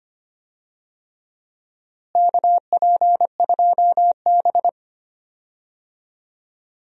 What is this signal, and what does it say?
Morse code, heard as a keyed tone, about 25 wpm, 700 Hz. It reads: KP2B
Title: DXCC-100 Call Signs - 4 Characters - Repeated 3x - 25wpm